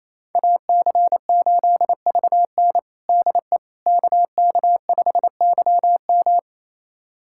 Morse code, heard as a keyed tone, about 27 wpm, 700 Hz. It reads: AC8VN DE KK5YM
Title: USA Callsign DE Callsign - 5 Characters - 28wpm